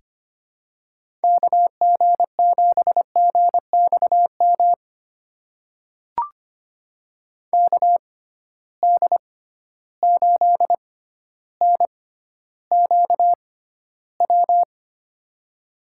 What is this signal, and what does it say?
Morse code, heard as a keyed tone, about 25 wpm, spaced as for 13 wpm, 700 Hz. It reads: KG7GXM KD8NQW
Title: USA Call Signs - 6 Characters - 6x Character Spacing - 25wpm